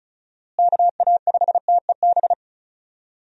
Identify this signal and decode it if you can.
Morse code, heard as a keyed tone, about 34 wpm, 700 Hz. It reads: KA5TEB
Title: USA Call Signs - 6 Characters - Repeated 2x - 35wpm